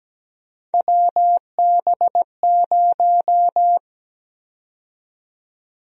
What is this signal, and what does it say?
Morse code, heard as a keyed tone, about 17 wpm, 700 Hz. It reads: WB0
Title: USA Call Sign Prefixes - 17wpm